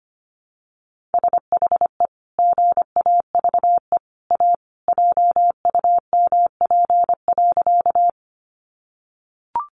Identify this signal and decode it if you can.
Morse code, heard as a keyed tone, about 25 wpm, 700 Hz. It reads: SHE GAVE A JUMP.
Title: Sentences from Top 1300 Words 25wpm